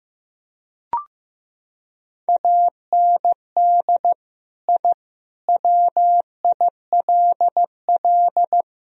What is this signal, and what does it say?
Morse code, heard as a keyed tone, about 15 wpm, 700 Hz. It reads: AND I WILL
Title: Sentences from Top 1000 Words - Review Part 1 - 15wpm